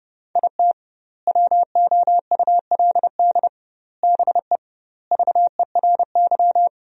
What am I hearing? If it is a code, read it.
Morse code, heard as a keyed tone, about 30 wpm, 700 Hz. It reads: IT WOULD BE VERY
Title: 2x Word-Spacing - Sentences from Top 200 Words - 30wpm